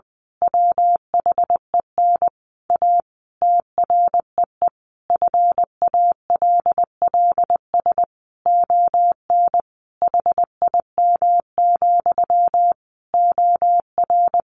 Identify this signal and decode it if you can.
Morse code, heard as a keyed tone, about 20 wpm, 700 Hz. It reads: WHEN A TREE FALLS ON HIM, OR
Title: A Sound Of Thunder 20wpm